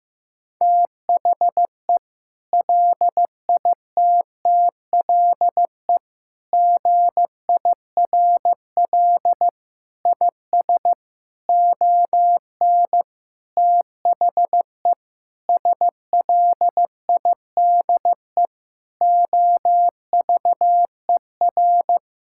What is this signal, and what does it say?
Morse code, heard as a keyed tone, about 15 wpm, 700 Hz. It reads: THE LITTLE GIRL IS ON THE SLIDE OVER
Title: Sentences from Top 1600 Words 15wpm